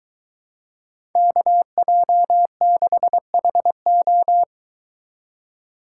Morse code, heard as a keyed tone, about 22 wpm, 700 Hz. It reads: KJ6HO